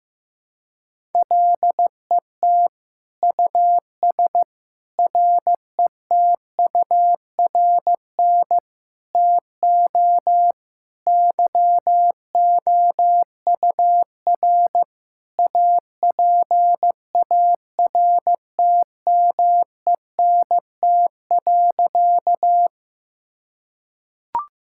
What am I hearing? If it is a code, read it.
Morse code, heard as a keyed tone, about 15 wpm, 700 Hz. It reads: LET US RETURN TO YOUR APARTMENT.
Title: Sentences from Top 1000 Words - Review Part 6 - 15wpm